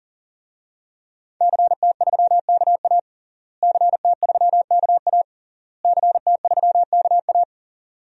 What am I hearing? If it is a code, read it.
Morse code, heard as a keyed tone, about 40 wpm, 700 Hz. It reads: CT3KA CT3KA CT3KA
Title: DXCC-100 Call Signs - 5 Characters - Repeated 3x - 40wpm